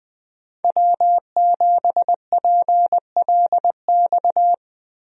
Morse code, heard as a keyed tone, about 20 wpm, 700 Hz. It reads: W7PLX
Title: USA Call Signs - 5 Characters - Repeated 3x - 20wpm